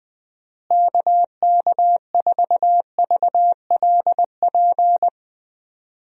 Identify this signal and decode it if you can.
Morse code, heard as a keyed tone, about 20 wpm, 700 Hz. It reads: KK4VLP
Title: USA Call Signs - 6 Characters - Repeated 3x - 20wpm